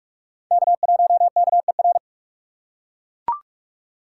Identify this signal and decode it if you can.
Morse code, heard as a keyed tone, about 42 wpm, 700 Hz. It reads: K1KER
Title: USA Call Signs - 4x Character Spacing - 45wpm